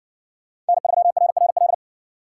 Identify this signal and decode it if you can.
Morse code, heard as a keyed tone, about 59 wpm, 700 Hz. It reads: N3RRL